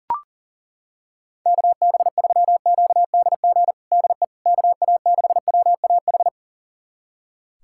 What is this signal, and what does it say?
Morse code, heard as a keyed tone, about 40 wpm, 690 Hz. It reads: KB3QDG DE KA6WAH
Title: USA Callsign DE Callsign - 6 Characters - 40wpm